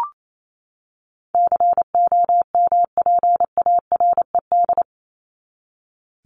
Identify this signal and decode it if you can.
Morse code, heard as a keyed tone, about 28 wpm, 700 Hz. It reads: COMPARED
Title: ED Suffix - Top 100 Words with Base Words 28wpm